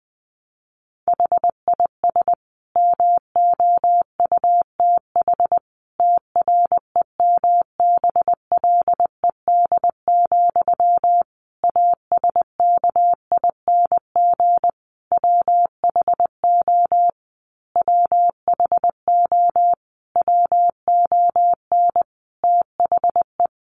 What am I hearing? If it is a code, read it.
Morse code, heard as a keyed tone, about 20 wpm, 700 Hz. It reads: HIS MOUTH TREMBLED, ASKING WHO WHO WON THE
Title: A Sound Of Thunder 20wpm